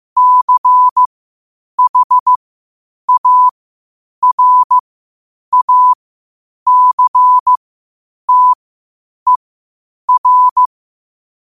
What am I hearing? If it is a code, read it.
Morse code, heard as a keyed tone, about 15 wpm, spaced as for 8 wpm, 1000 Hz. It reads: CHARACTER
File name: qotdfiles / 8wpm